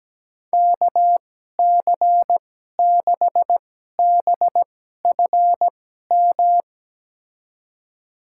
Morse code, heard as a keyed tone, about 17 wpm, spaced as for 12 wpm, 700 Hz. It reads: KC6BFM
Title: USA Call Signs - 6 Characters - 2x Character Spacing - 17wpm